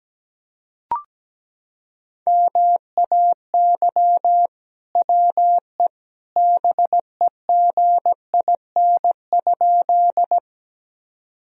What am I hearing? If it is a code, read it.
Morse code, heard as a keyed tone, about 17 wpm, 700 Hz. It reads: MAY WE BEGIN?
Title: Sentences from Top 1000 Words - Review Part 1 - 17wpm